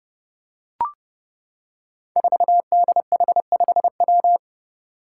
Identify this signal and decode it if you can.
Morse code, heard as a keyed tone, about 30 wpm, 700 Hz. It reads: 4DH5W
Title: Five Challenging Letter-Number Combinations 30wpm